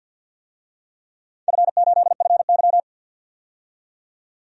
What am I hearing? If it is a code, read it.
Morse code, heard as a keyed tone, about 50 wpm, 700 Hz. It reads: F8FY
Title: DXCC-100 Call Signs - 4 Characters - 50wpm